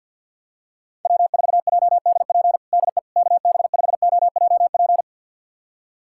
Morse code, heard as a keyed tone, about 51 wpm, 700 Hz. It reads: W4JDP DE KB5OJP
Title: USA Callsign DE Callsign 50wpm